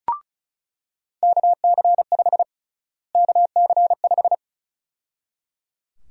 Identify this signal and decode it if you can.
Morse code, heard as a keyed tone, about 34 wpm, 700 Hz. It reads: KC5 KC5
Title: USA Call Sign Prefixes - Repeated 2x - 35wpm